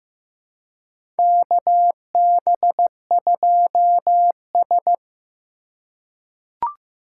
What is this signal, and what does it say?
Morse code, heard as a keyed tone, about 15 wpm, 700 Hz. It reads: KB2S
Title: K1USN SST QSO Practice - 15wpm